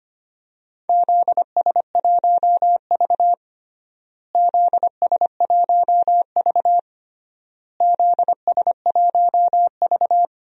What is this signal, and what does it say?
Morse code, heard as a keyed tone, about 25 wpm, 700 Hz. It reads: ZS1V ZS1V ZS1V
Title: DXCC-100 Call Signs - 4 Characters - Repeated 3x - 25wpm